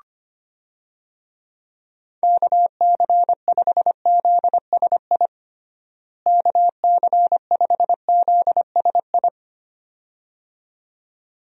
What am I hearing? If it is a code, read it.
Morse code, heard as a keyed tone, about 25 wpm, 700 Hz. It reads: KC5ZSI KC5ZSI
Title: USA Call Signs - 6 Characters - Repeated 2x - 25wpm